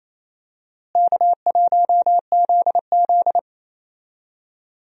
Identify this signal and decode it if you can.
Morse code, heard as a keyed tone, about 28 wpm, 700 Hz. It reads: K1ZZ
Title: USA Call Signs - 4 Characters - Repeated 3x - 28wpm